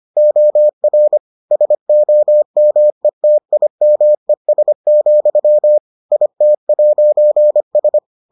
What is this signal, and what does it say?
Morse code, heard as a keyed tone, about 25 wpm, 600 Hz. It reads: OR SOMETIMES, IT'S